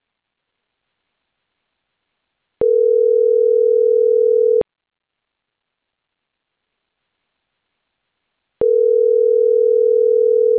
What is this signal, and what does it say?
Ringback tone (North America), 440 + 480 Hz.